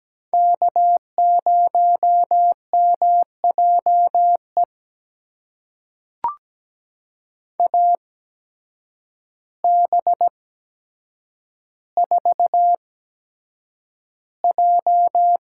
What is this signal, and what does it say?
Morse code, heard as a keyed tone, about 17 wpm, spaced as for 8 wpm, 700 Hz. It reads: K0MJE AB4J
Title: USA Call Signs - 5 Characters - 8x Character Spacing - 17wpm